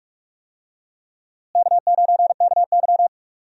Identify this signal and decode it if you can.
Morse code, heard as a keyed tone, about 41 wpm, 700 Hz. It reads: K9KY
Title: USA Call Signs - 4 Characters - Repeated 2x - 45wpm